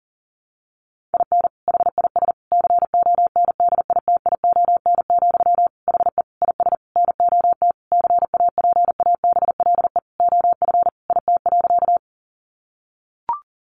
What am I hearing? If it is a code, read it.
Morse code, heard as a keyed tone, about 40 wpm, 700 Hz. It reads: IN HIS CONDITION, HE IS NOT CAPABLE OF IT.
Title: Sentences from Top 1200 Words 40wpm